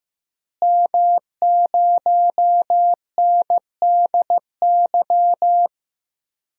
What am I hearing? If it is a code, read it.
Morse code, heard as a keyed tone, about 15 wpm, 700 Hz. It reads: M0NDY
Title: DXCC-100 Call Signs - 5 Characters - Repeated 3x - 15wpm